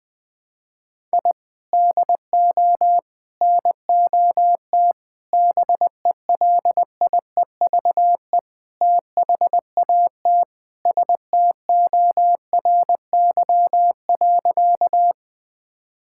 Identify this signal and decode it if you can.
Morse code, heard as a keyed tone, about 20 wpm, 700 Hz. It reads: I DO NOT BELIEVE THAT STORY.
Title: Sentences with 5-Word Phrases - 20wpm